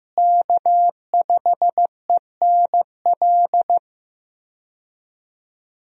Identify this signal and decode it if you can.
Morse code, heard as a keyed tone, about 15 wpm, 700 Hz. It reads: K5ENL
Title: USA Call Signs - 5 Characters - Repeated 3x - 15wpm